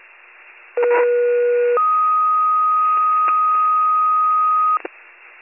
Two-tone page, heard as a one-second tone, then a three-second tone, about 480 then 1250 Hz. It is heard from 0.8 to 4.8 s.